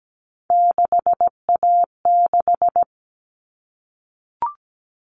Morse code, heard as a keyed tone, about 17 wpm, 700 Hz. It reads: 6A6
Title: All Cut Numbers - 3-Digit 17wpm